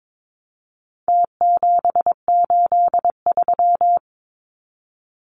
Morse code, heard as a keyed tone, about 22 wpm, 700 Hz. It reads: T783
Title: Numbers - Contest Style - 4-Digit 22wpm